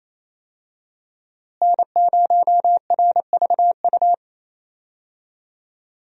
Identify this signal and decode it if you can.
Morse code, heard as a keyed tone, about 28 wpm, 700 Hz. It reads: N0RVU
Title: USA Call Signs - 5 Characters - Repeated 3x - 28wpm